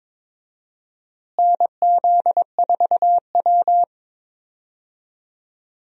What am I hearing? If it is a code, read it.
Morse code, heard as a keyed tone, about 21 wpm, 700 Hz. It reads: NZ4W
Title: USA Call Signs - 4 Characters - Repeated 3x - 22wpm